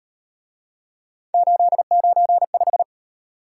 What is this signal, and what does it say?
Morse code, heard as a keyed tone, about 34 wpm, 700 Hz. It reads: 895